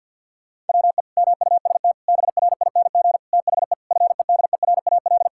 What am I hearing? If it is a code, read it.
Morse code, heard as a keyed tone, about 48 wpm, 700 Hz. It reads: WE MUST BRING THE FEDERAL
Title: Sentences from Top 1000 Words - Review Part 6 - 50wpm